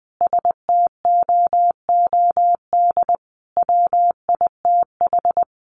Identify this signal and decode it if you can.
Morse code, heard as a keyed tone, about 20 wpm, 700 Hz. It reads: STOOD WITH
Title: A Sound Of Thunder 20wpm